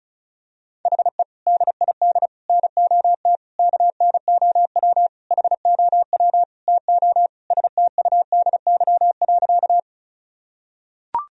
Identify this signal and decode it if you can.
Morse code, heard as a keyed tone, about 36 wpm, 700 Hz. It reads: HE DID NOT KNOW HOW TO STUDY.
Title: Sentences from Top 1000 Words - Review Part 1 - 35wpm